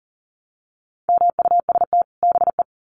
Morse code, heard as a keyed tone, about 40 wpm, 700 Hz. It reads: MUST BE
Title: Sets of 2 Words 40wpm